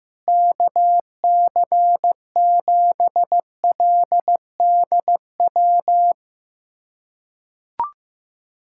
Morse code, heard as a keyed tone, about 15 wpm, 700 Hz. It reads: KC7LDW
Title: USA Callsign DE Callsign - 6 Characters - 15wpm